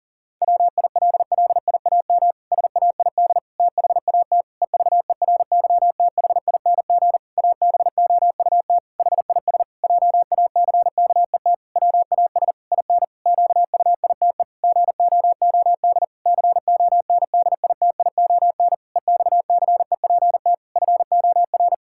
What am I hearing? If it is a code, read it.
Morse code, heard as a keyed tone, about 40 wpm, 700 Hz. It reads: WILLIAM SAID THAT EVERYTHING ABOUT HIS JACKET WAS IN QUITE GOOD CONDITION EXCEPT FOR
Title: Pangram Challenge v2 - 40wpm